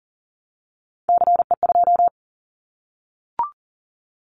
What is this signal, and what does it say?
Morse code, heard as a keyed tone, about 40 wpm, 700 Hz. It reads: CE2
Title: Callsign Prefixes for Top 100 DXCC - 40wpm